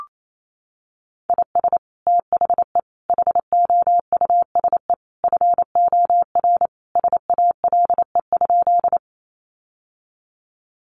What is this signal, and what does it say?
Morse code, heard as a keyed tone, about 28 wpm, 700 Hz. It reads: IS THE HOUSE FOR SALE?
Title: Sentences from Top 1400 Words 28wpm